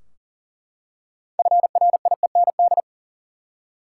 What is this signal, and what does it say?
Morse code, heard as a keyed tone, about 40 wpm, 700 Hz. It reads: FRIEND
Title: Six Letter Words - 40wpm